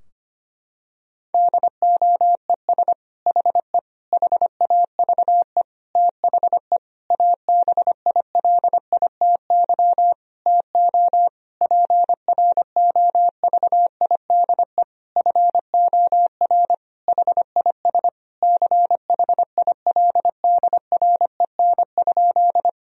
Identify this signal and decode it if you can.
Morse code, heard as a keyed tone, about 25 wpm, 700 Hz. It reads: DOES HE HAVE THE ABILITY TO PROVIDE FOR HIS CHILDREN?
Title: Sentences from Top 1000 Words - Review Part 3 - 25wpm